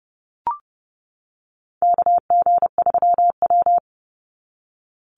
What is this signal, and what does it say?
Morse code, heard as a keyed tone, about 30 wpm, 700 Hz. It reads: KG3W
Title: NA SOTA Chaser Callsigns 30wpm